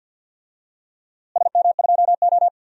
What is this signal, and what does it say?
Morse code, heard as a keyed tone, about 51 wpm, 700 Hz. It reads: SM2O
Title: DXCC-100 Call Signs - 4 Characters - Repeated 2x - 50wpm